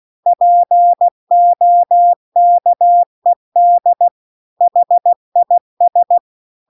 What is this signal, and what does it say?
Morse code, heard as a keyed tone, about 16 wpm, 700 Hz. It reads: POKED HIS
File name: CW-The-Big-Twoheaded-River-2-16WPM